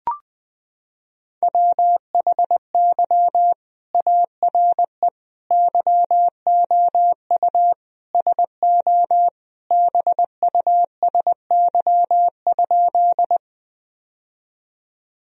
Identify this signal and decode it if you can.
Morse code, heard as a keyed tone, about 20 wpm, 700 Hz. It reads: WHY ARE YOU SO BUSY?
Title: Sentences from Top 1000 Words - Review Part 3 - 20wpm